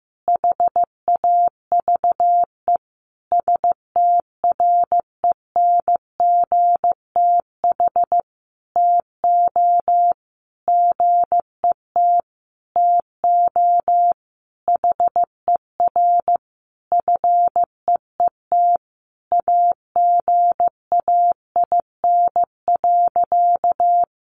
Morse code, heard as a keyed tone, about 15 wpm, 700 Hz. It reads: HAVE STRENGTH TO GET TO HER FEET AGAIN.
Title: Sentences from Top 1300 Words 15wpm